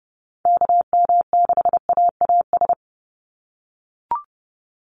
Morse code, heard as a keyed tone, about 30 wpm, 700 Hz. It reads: KM6AAS
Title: USA Call Signs 30wpm